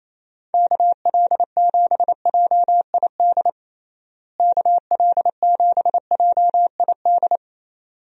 Morse code, heard as a keyed tone, about 28 wpm, 700 Hz. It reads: KL7JID KL7JID
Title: DXCC-100 Call Signs - 6 Characters - Repeated 2x - 28wpm